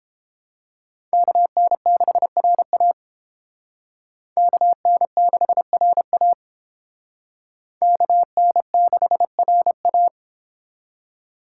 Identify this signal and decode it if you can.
Morse code, heard as a keyed tone, about 31 wpm, 700 Hz. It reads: KN6RA KN6RA KN6RA